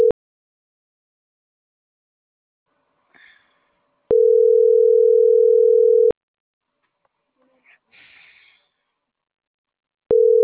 Ringback tone (North America), 440 + 480 Hz.